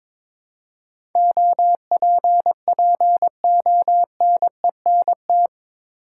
Morse code, heard as a keyed tone, about 22 wpm, 700 Hz. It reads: OPPONENT
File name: practice / 22wpm / Eight Letter Words - 22wpm